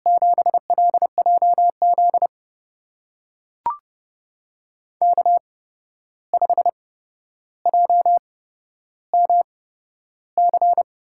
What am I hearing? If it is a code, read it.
Morse code, heard as a keyed tone, about 30 wpm, spaced as for 12 wpm, 700 Hz. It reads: 7LJZ K5JMC
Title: USA Call Signs - 5 Characters - 8x Character Spacing - 30wpm